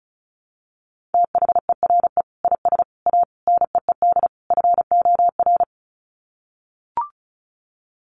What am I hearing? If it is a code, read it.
Morse code, heard as a keyed tone, about 34 wpm, 700 Hz. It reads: THERE IS A NEED FOR
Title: Sets of 5 Words Encore2 35wpm